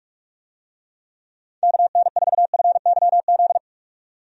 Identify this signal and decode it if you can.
Morse code, heard as a keyed tone, about 46 wpm, 700 Hz. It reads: KN4FYZ
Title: USA Call Signs - 6 Characters - 45wpm